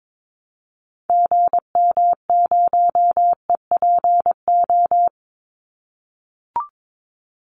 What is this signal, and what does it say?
Morse code, heard as a keyed tone, about 22 wpm, 700 Hz. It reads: GM0EPO
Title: Call Signs - 22wpm